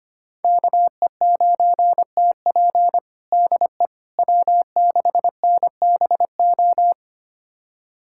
Morse code, heard as a keyed tone, about 25 wpm, 700 Hz. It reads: KE9TP DE W6NBO
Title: USA Callsign DE Callsign - 5 Characters - 25wpm